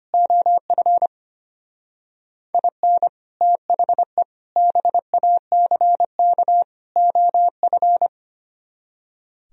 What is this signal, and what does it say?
Morse code, heard as a keyed tone, about 31 wpm, 700 Hz. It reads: OF IN THE BACK OF
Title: Speed-Racing - Sets of 4 Words 25wpm